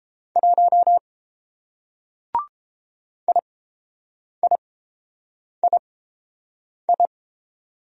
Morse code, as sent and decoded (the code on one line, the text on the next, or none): .---- / .. .. .. ..
1 IIII